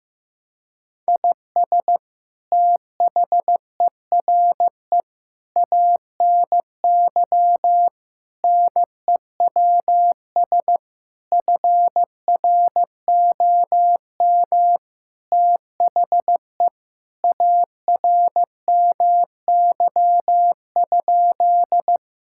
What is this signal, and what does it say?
Morse code, heard as a keyed tone, about 16 wpm, 700 Hz. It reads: IS THERE ANY NEWS FROM THE ARMY?
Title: Sentences from Top 1000 Words - Review Part 5 - 15wpm